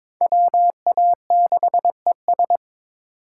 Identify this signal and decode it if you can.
Morse code, heard as a keyed tone, about 22 wpm, 700 Hz. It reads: WA6ES
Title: USA Call Signs - 5 Characters - 2x Character Spacing - 22wpm